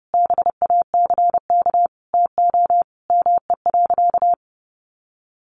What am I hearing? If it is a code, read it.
Morse code, heard as a keyed tone, about 30 wpm, 700 Hz. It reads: BACK TO ME.
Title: Sentences from Top 1100 Words 30wpm